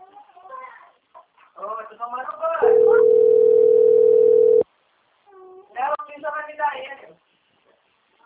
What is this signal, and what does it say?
Ringback tone (North America), 440 + 480 Hz, with speech.
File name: out- - - - - .13152